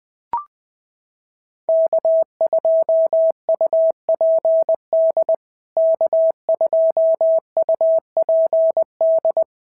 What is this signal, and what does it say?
Morse code, heard as a keyed tone, about 20 wpm, 650 Hz. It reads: K2UPD K2UPD
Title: POTA QSO Practice - 20wpm